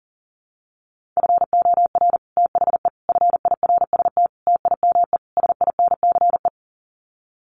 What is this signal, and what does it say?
Morse code, heard as a keyed tone, about 40 wpm, 700 Hz. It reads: FOR THE FIRST TIME SINCE
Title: Sets of 5 Words 40wpm